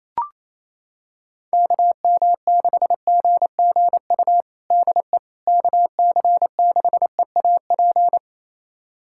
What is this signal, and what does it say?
Morse code, heard as a keyed tone, about 28 wpm, 700 Hz. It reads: KM6GGU DE KC6EAP
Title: USA Callsign DE Callsign 28wpm